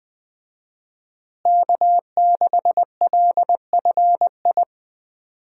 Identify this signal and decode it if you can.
Morse code, heard as a keyed tone, about 20 wpm, 700 Hz. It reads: K6LFI
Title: USA Call Signs - 5 Characters - Repeated 3x - 20wpm